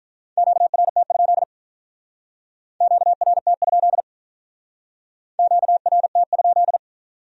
Morse code, heard as a keyed tone, about 43 wpm, 700 Hz. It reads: QRT? QRT? QRT?
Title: Speed-Racing Q-Codes - 35wpm